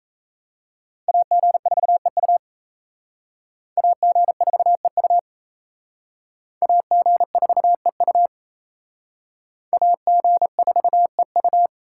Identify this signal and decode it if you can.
Morse code, heard as a keyed tone, about 35 wpm, 700 Hz. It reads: AG4EU AG4EU AG4EU AG4EU
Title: Speed Racing - USA 2x2 Call Signs 28wpm